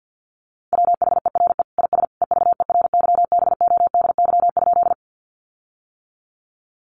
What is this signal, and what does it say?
Morse code, heard as a keyed tone, about 50 wpm, 700 Hz. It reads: WHERE IS EVERYBODY?
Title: Sentences from Top 1400 Words 50wpm